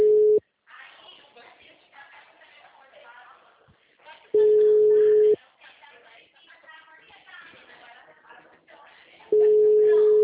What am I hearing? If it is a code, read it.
Ringback tone (Europe, 425 Hz), with speech.
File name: out- -2008- - - .207